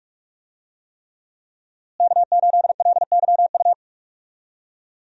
Morse code, heard as a keyed tone, about 41 wpm, 700 Hz. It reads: K8LYU
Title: USA Call Signs - 5 Characters - Repeated 3x - 45wpm